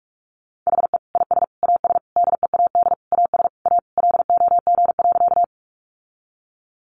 Morse code, heard as a keyed tone, about 44 wpm, 700 Hz. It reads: HE IS AS DEAD AS A LOG.
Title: Sentences from Top 1500 Words 45wpm